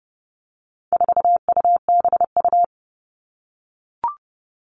Morse code, heard as a keyed tone, about 30 wpm, 700 Hz. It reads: 4UBU
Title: All Cut Numbers - 4-Digit 30wpm